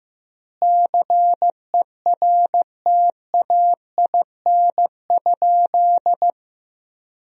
Morse code, heard as a keyed tone, about 15 wpm, 700 Hz. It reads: CERTAIN?
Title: 3x Word-Spacing - Sentences from Top 500 Words - 15wpm